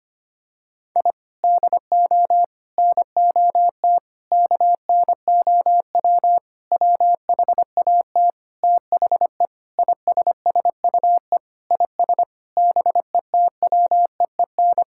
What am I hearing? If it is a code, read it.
Morse code, heard as a keyed tone, about 25 wpm, 700 Hz. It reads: I DO NOT KNOW WHAT THE ISSUE IS BETWEEN
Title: Sentences from Top 1000 Words - Review Part 6 - 25wpm